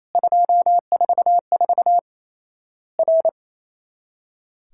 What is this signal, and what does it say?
Morse code, heard as a keyed tone, about 28 wpm, 700 Hz. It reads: 244 R
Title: Worked All Europe DX Contest - QTC Exchange - 28wpm